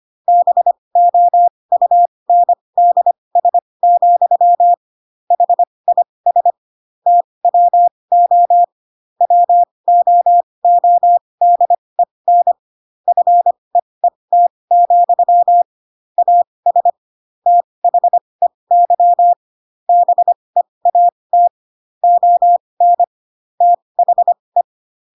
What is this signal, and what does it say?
Morse code, heard as a keyed tone, about 25 wpm, spaced as for 20 wpm, 700 Hz. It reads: BOUNDS, HIS TWO WOODEN FEET, AS THEY BEAT ON THE